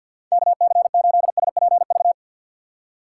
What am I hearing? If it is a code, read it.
Morse code, heard as a keyed tone, about 52 wpm, 700 Hz. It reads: KC8SPV
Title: USA Call Signs - 6 Characters - 4x Character Spacing - 50wpm